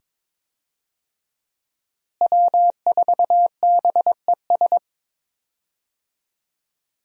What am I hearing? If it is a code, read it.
Morse code, heard as a keyed tone, about 22 wpm, 700 Hz. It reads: W4BES
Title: USA Call Signs - 5 Characters - Repeated 2x - 22wpm